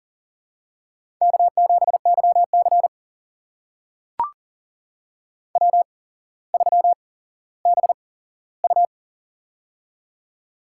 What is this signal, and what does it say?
Morse code, heard as a keyed tone, about 40 wpm, spaced as for 17 wpm, 700 Hz. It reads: K7YC W3BU
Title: USA Call Signs - 4 Characters - 8x Character Spacing - 40wpm